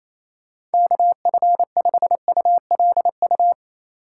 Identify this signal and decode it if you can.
Morse code, heard as a keyed tone, about 36 wpm, 700 Hz. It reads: KF5ULU